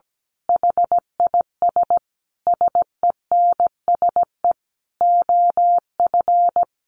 Morse code, heard as a keyed tone, about 17 wpm, 700 Hz. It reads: HIS SENSE OF
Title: Sentences from Top 1400 Words 17wpm